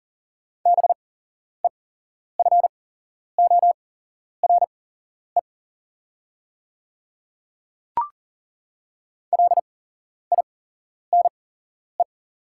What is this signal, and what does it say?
Morse code, heard as a keyed tone, about 40 wpm, spaced as for 11 wpm, 700 Hz. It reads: BEFORE LINE
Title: 8x Character Spacing - Top 300 Words - 40wpm